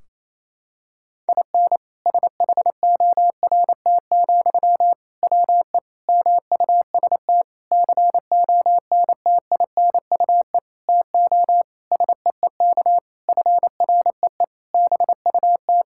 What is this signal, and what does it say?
Morse code, heard as a keyed tone, about 28 wpm, 700 Hz. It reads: IN SHORT, WE MUST CONTINUE TO SEEK FREE BUT
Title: Sentences from Top 1600 Words 28wpm